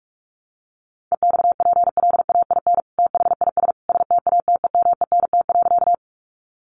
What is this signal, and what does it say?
Morse code, heard as a keyed tone, about 44 wpm, 700 Hz. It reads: EXPLAIN THIS STATEMENT.
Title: Sentences from Top 1200 Words 45wpm